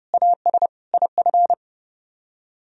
Morse code, heard as a keyed tone, about 36 wpm, 700 Hz. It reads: AS IF
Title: Speed-Racing - Sets of 4 Words 30wpm